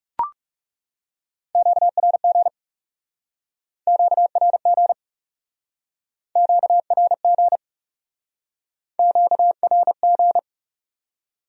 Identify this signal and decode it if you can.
Morse code, heard as a keyed tone, about 39 wpm, 700 Hz. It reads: QRG QRG QRG QRG